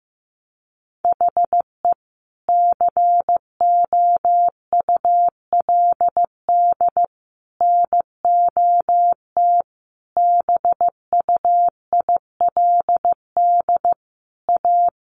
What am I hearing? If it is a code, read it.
Morse code, heard as a keyed tone, about 15 wpm, 700 Hz. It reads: HE COULD NOT BUILD A
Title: Sentences from Top 1100 Words 15wpm